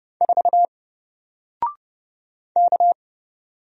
Morse code, heard as a keyed tone, about 30 wpm, 700 Hz. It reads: 4 K
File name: Single-Letter-Number-30wpm